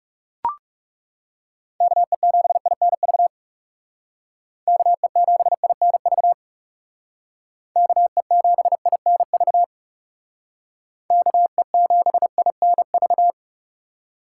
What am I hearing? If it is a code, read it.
Morse code, heard as a keyed tone, about 38 wpm, 700 Hz. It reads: KE7INV KE7INV KE7INV KE7INV